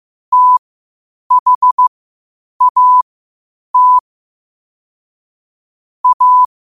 Morse code, heard as a keyed tone, about 15 wpm, spaced as for 8 wpm, 1000 Hz. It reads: THAT A
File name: qotdfiles / 8wpm